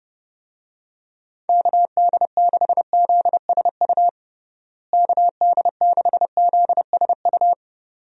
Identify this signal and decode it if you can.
Morse code, heard as a keyed tone, about 30 wpm, 700 Hz. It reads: KD6ZSU KD6ZSU
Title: USA Call Signs - 6 Characters - Repeated 2x - 30wpm